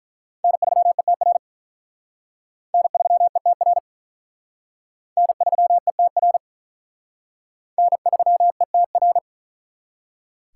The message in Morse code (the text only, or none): N3ETR N3ETR N3ETR N3ETR